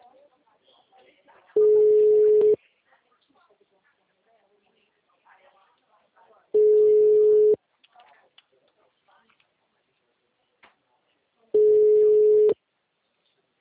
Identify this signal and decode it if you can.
Ringback tone (Europe, 425 Hz), with speech.
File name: out- -2009- - - .35565